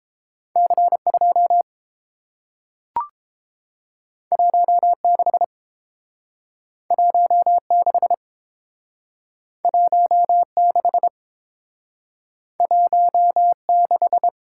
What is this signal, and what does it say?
Morse code, heard as a keyed tone, about 30 wpm, 700 Hz. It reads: C2 16 16 16 16